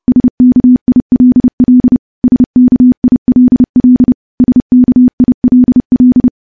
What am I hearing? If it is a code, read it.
Morse code, heard as a keyed tone, about 30 wpm, 250 Hz. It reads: SKILL SKILL SKILL
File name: morse / morse-training-30WPM